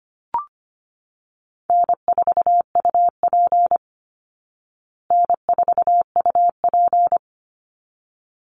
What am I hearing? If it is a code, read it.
Morse code, heard as a keyed tone, about 25 wpm, 700 Hz. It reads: N4UP N4UP
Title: Call Signs Repeated 3x - 25wpm